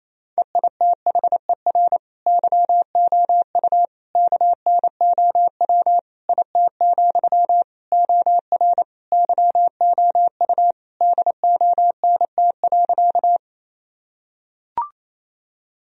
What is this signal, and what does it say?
Morse code, heard as a keyed tone, about 28 wpm, 700 Hz. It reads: EITHER YOU KNOW IT, OR YOU DONT.
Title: Sentences from Top 1000 Words - Review Part 4 - 28wpm